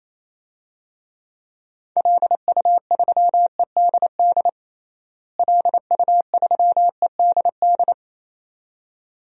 Morse code, heard as a keyed tone, about 27 wpm, 700 Hz. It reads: LU3EDD LU3EDD
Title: DXCC-100 Call Signs - 6 Characters - Repeated 2x - 28wpm